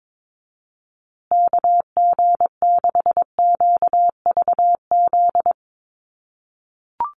Morse code, heard as a keyed tone, about 22 wpm, 700 Hz. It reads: KG6QVZ
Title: USA Call Signs Repeated 3x 22wpm